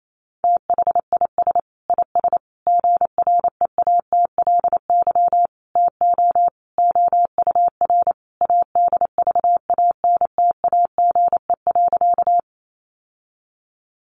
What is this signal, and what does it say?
Morse code, heard as a keyed tone, about 28 wpm, 700 Hz. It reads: THIS IS GREATLY TO OUR ADVANTAGE.
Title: Sentences from Top 1400 Words 28wpm